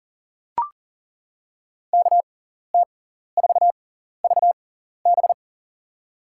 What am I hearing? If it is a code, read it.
Morse code, heard as a keyed tone, about 40 wpm, spaced as for 14 wpm, 700 Hz. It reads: KT4VB
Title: USA Call Signs - 5 Characters - 6x Character Spacing - 40wpm